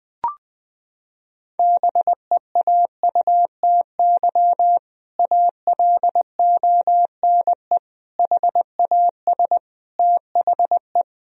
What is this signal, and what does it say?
Morse code, heard as a keyed tone, about 20 wpm, 700 Hz. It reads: BEAUTY ALONE HAS THE
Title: Sentences from Top 1000 Words - Review Part 2 - 20wpm